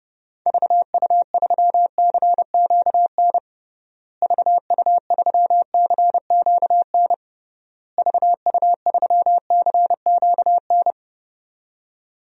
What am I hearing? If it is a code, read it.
Morse code, heard as a keyed tone, about 30 wpm, 700 Hz. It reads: VU3CQN VU3CQN VU3CQN
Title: DXCC-100 Call Signs - 6 Characters - Repeated 3x - 30wpm